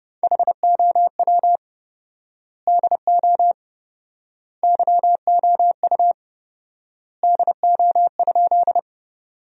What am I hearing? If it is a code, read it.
Morse code, heard as a keyed tone, about 30 wpm, 700 Hz. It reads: HOW DO YOU DO?
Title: 4x Word-Spacing - Sentences from Top 100 Words - 30wpm